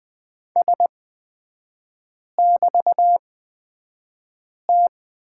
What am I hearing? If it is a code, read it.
Morse code, heard as a keyed tone, about 20 wpm, spaced as for 5 wpm, 700 Hz. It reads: S<BT>T
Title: 224 Lesson 36e - BT - Three Characters 20wpm